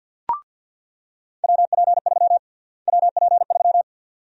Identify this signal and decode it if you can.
Morse code, heard as a keyed tone, about 47 wpm, 700 Hz. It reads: WP3 WP3
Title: USA Call Sign Prefixes - Repeated 2x - 50wpm